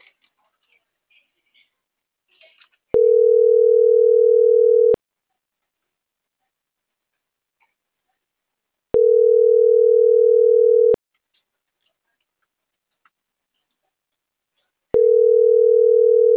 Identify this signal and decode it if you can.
Ringback tone (North America), 440 + 480 Hz.